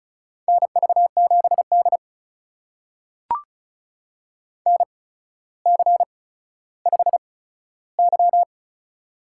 Morse code, heard as a keyed tone, about 36 wpm, spaced as for 15 wpm, 700 Hz. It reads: NV7D NC5Y
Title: USA Call Signs - 8x Character Spacing - 35wpm